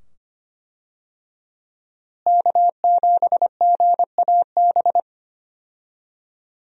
Morse code, heard as a keyed tone, about 25 wpm, 700 Hz. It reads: K7GAB
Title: USA Call Signs - 5 Characters - Repeated 2x - 25wpm